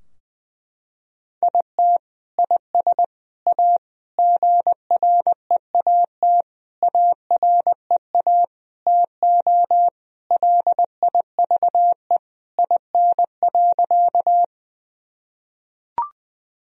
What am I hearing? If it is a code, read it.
Morse code, heard as a keyed tone, about 20 wpm, 700 Hz. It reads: IT IS A GREAT AREA TO LIVE IN.
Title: Sentences from Top 1000 Words - Review Part 2 - 20wpm